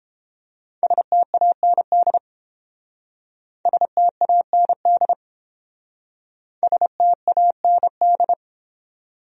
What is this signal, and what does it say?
Morse code, heard as a keyed tone, about 31 wpm, 700 Hz. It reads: STAND STAND STAND